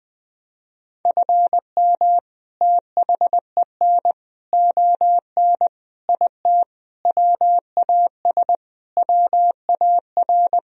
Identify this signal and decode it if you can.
Morse code, heard as a keyed tone, about 20 wpm, 700 Hz. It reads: FM THEN ON IT WAS WAR
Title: Sentences from Top 400 Words and Abbreviations 20wpm